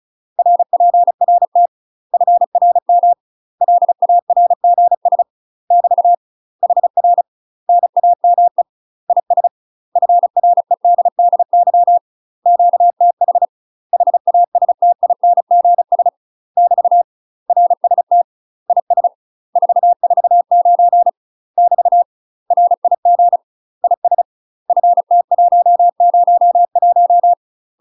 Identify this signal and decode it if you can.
Morse code, heard as a keyed tone, about 36 wpm, 700 Hz. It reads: RPRT FRM LARGS <BT> HR NAME IS FREDDY QTH HASTINGS <BT> RST IS 449 <BT> RIG IS FT101
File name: UK MASTER QSOs TRAINING / 35WPM / UK MASTER QSOs 0011